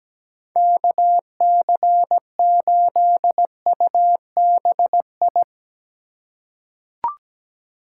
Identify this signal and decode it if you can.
Morse code, heard as a keyed tone, about 17 wpm, 700 Hz. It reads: KC8UBI
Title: USA Call Signs - 6 Characters - 4x Character Spacing - 17wpm